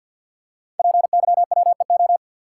Morse code, heard as a keyed tone, about 53 wpm, 700 Hz. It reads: PYWEO
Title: Code Groups - 5 Letters - 50wpm